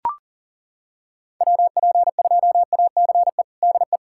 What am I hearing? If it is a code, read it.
Morse code, heard as a keyed tone, about 40 wpm, 700 Hz. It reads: WP2ACE DE
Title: USA Callsign DE Callsign - 6 Characters - 40wpm